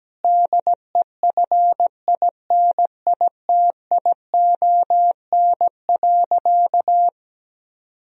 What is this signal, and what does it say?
Morse code, heard as a keyed tone, about 17 wpm, 700 Hz. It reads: DEFINITION.
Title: Sentences from Top 1000 Words - Review Part 4 - 17wpm